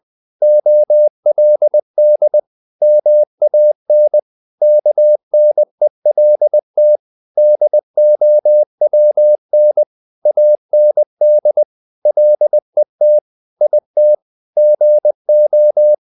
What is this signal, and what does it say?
Morse code, heard as a keyed tone, about 20 wpm, 600 Hz. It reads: OLD MAN KNELT DOWN AND LET IT GO
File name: the old man and the sea part 2 20wpm